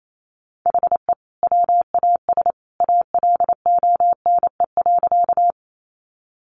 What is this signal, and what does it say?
Morse code, heard as a keyed tone, about 28 wpm, 700 Hz. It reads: HE WAS ALONE.
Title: Sentences from Top 1000 Words 28wpm